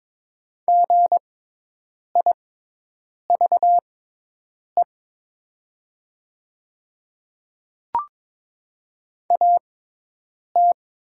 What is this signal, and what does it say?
Morse code, heard as a keyed tone, about 21 wpm, spaced as for 8 wpm, 700 Hz. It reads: GIVE AT
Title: 6x Character Spacing - Top 300 Words - 22wpm